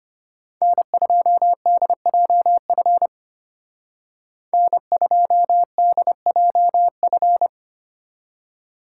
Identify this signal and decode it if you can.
Morse code, heard as a keyed tone, about 33 wpm, 700 Hz. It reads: N2DJF N2DJF